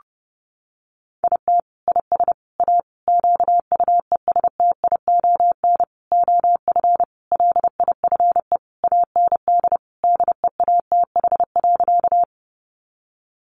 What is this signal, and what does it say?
Morse code, heard as a keyed tone, about 30 wpm, 700 Hz. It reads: IT IS A QUESTION OF LIFE AND DEATH.
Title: Sentences from Top 1100 Words 30wpm